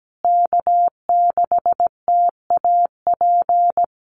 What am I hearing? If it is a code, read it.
Morse code, heard as a keyed tone, about 17 wpm, 700 Hz. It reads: K6TAP
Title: USA Call Signs Repeated 2x 17wpm